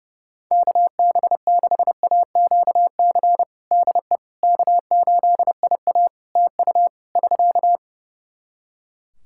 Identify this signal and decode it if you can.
Morse code, heard as a keyed tone, about 30 wpm, 700 Hz. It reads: KB6AQC DE K8IA TU <SK>
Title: K1USN SST QSO Practice 30wpm